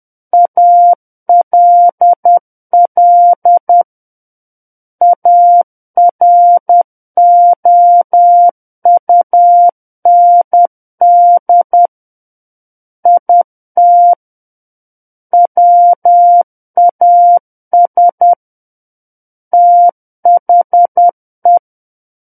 Morse code, heard as a keyed tone, about 10 wpm, 700 Hz. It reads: ALL AROUND IT WAS THE
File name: Book One / 01-Chapter 1 - 10 WPM